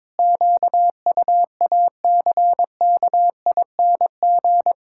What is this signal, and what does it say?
Morse code, heard as a keyed tone, about 22 wpm, 700 Hz. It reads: QUACKING
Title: Pangram Challenge v2 - 22wpm